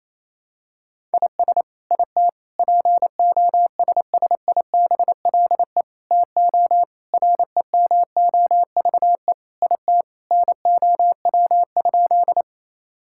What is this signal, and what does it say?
Morse code, heard as a keyed tone, about 27 wpm, 700 Hz. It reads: IS IT POSSIBLE TO REMOVE IT NOW?
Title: Sentences from Top 1000 Words - Review Part 5 - 28wpm